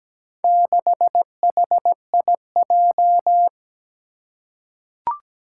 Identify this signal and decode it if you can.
Morse code, heard as a keyed tone, about 17 wpm, 700 Hz. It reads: 6HIJ
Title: Four Challenging Letter-Number Combinations 17wpm